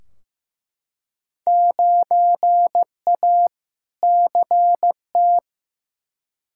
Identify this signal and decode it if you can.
Morse code, heard as a keyed tone, about 15 wpm, 700 Hz. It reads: 9A CT
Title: Field Day Exchanges Encore - 15wpm